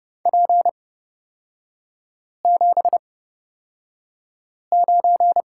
Morse code, heard as a keyed tone, about 30 wpm, spaced as for 5 wpm, 700 Hz. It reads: P79
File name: Lesson 37e - Letter x - Three Characters 30wpm